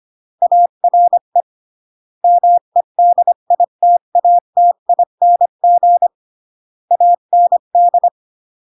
Morse code, heard as a keyed tone, about 25 wpm, 700 Hz. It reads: ARE MEDITATING AND